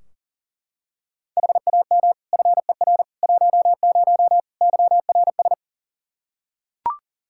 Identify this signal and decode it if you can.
Morse code, heard as a keyed tone, about 40 wpm, 700 Hz. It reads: HAM FER 10 YRS
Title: K1USN SST QSO Practice 40wpm